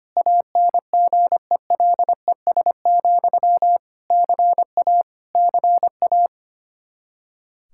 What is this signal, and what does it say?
Morse code, heard as a keyed tone, about 25 wpm, 700 Hz. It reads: ANGELES, CA CA
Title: Practice QSO - Exchange 1 - Beginning - 25wpm